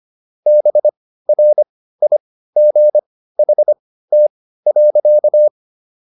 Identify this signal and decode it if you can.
Morse code, heard as a keyed tone, about 25 wpm, spaced as for 15 wpm, 600 Hz. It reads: BRIGHT.